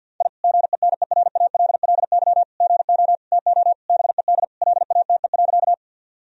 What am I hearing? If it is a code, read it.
Morse code, heard as a keyed tone, about 50 wpm, 700 Hz. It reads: I GENERALLY GO TO BED LATE.
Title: Sentences from Top 1000 Words - Review Part 4 - 50wpm